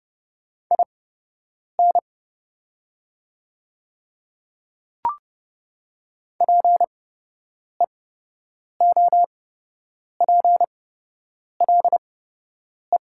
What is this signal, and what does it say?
Morse code, heard as a keyed tone, about 30 wpm, spaced as for 8 wpm, 700 Hz. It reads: IN PEOPLE
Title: 8x Character Spacing - Top 300 Words - 30wpm